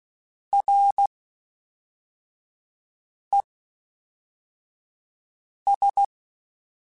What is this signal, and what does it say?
Morse code, heard as a keyed tone, about 16 wpm, 800 Hz. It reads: R E S
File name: Combinations / Combinations-mp3s / Combinations@10